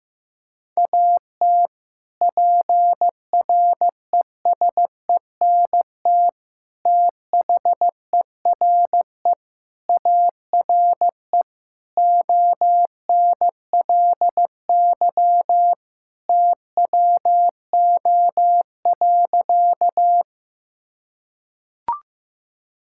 Morse code, heard as a keyed tone, about 16 wpm, 700 Hz. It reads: AT PRESENT THERE ARE ONLY TWO.
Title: Sentences from Top 1000 Words - Review Part 5 - 15wpm